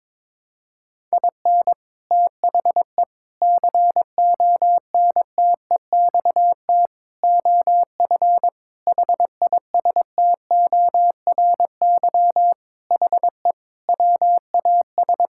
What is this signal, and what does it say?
Morse code, heard as a keyed tone, about 22 wpm, 700 Hz. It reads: IN THE CONTEXT OF HISTORY HE WAS
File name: practice / 22wpm / Sentences with 4-Word Phrases - 22wpm